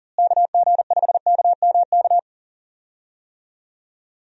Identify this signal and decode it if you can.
Morse code, heard as a keyed tone, about 40 wpm, 700 Hz. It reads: KG5KMK
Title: USA Call Signs - 6 Characters - Repeated 2x - 40wpm